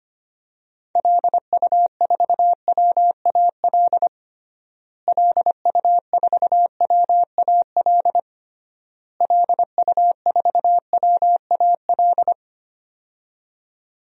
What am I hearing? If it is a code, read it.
Morse code, heard as a keyed tone, about 25 wpm, 700 Hz. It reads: LU4WAL LU4WAL LU4WAL
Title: DXCC-100 Call Signs - 6 Characters - Repeated 3x - 25wpm